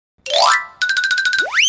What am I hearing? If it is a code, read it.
There is no code.